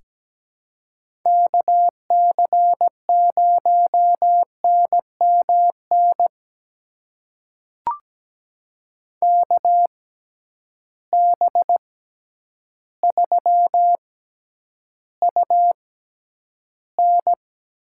Morse code, heard as a keyed tone, about 17 wpm, spaced as for 9 wpm, 700 Hz. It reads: KC0NMN KB3UN
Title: USA Call Signs - 6 Characters - 6x Character Spacing - 17wpm